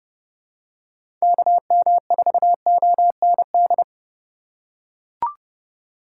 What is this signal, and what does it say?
Morse code, heard as a keyed tone, about 30 wpm, 700 Hz. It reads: KM4OND
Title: USA Call Signs - 6 Characters - 3x Character Spacing - 30wpm